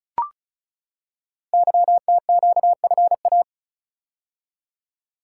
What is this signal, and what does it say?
Morse code, heard as a keyed tone, about 36 wpm, 690 Hz. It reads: YTQFA
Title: Code Groups - 5 Letters - 35wpm